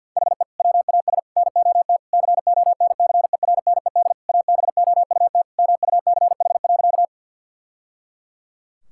Morse code, heard as a keyed tone, about 50 wpm, 700 Hz. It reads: HE WAS NOT CONCERNED ABOUT MUCH.
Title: Sentences from Top 1000 Words - Review Part 6 - 50wpm